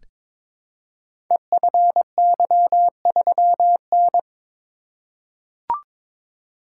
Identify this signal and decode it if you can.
Morse code, heard as a keyed tone, about 22 wpm, 700 Hz. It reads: EFY3N
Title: Code Groups - 5 Letter:Numbers - 22wpm